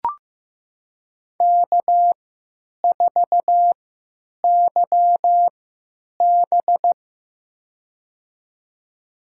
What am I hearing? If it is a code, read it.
Morse code, heard as a keyed tone, about 15 wpm, spaced as for 9 wpm, 700 Hz. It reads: K4YB
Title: USA Call Signs - 4 Characters - 3x Character Spacing - 15wpm